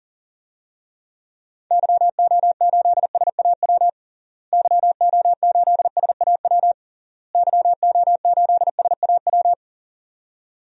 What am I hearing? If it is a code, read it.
Morse code, heard as a keyed tone, about 40 wpm, 700 Hz. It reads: YO8SAW YO8SAW YO8SAW
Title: DXCC-100 Call Signs - 6 Characters - Repeated 3x - 40wpm